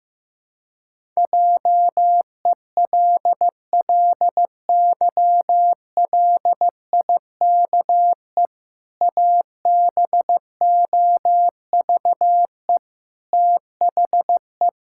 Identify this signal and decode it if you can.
Morse code, heard as a keyed tone, about 15 wpm, 700 Hz. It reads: JELLYLIKE ABOVE THE
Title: Pangram Challenge v2 - 15wpm